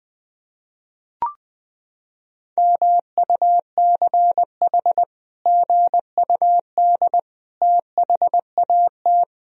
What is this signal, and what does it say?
Morse code, heard as a keyed tone, about 20 wpm, 700 Hz. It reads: MUCH GUD THAT
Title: Sentences from Top 200 Words and Abbreviations 20wpm